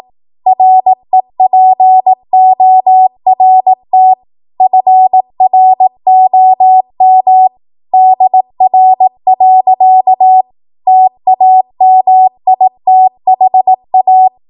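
Morse code, heard as a keyed tone, about 18 wpm, 750 Hz. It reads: REPORT FROM DR. TAMITHA
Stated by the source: ARRL 18 wpm Morse code practice archive transmission.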